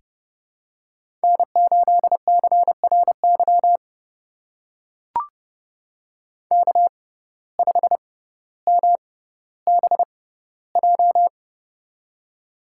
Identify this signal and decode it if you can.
Morse code, heard as a keyed tone, about 31 wpm, spaced as for 16 wpm, 700 Hz. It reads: N8CRY K5MBJ
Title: USA Call Signs - 5 Characters - 6x Character Spacing - 30wpm